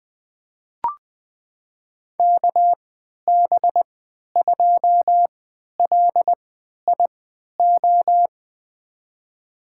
Morse code, heard as a keyed tone, about 21 wpm, spaced as for 11 wpm, 700 Hz. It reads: KB2LIO
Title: USA Call Signs - 3x Character Spacing - 20wpm